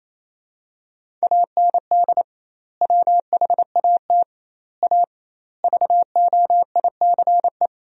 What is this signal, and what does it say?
Morse code, heard as a keyed tone, about 28 wpm, 700 Hz. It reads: AND WHAT A VOICE
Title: 2x Word-Spacing - Sentences from Top 500 Words - 28wpm